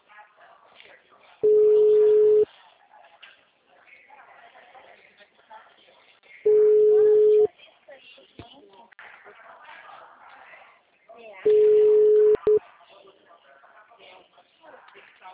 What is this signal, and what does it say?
Ringback tone (Europe, 425 Hz), with speech.
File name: out- -2003- - - .81801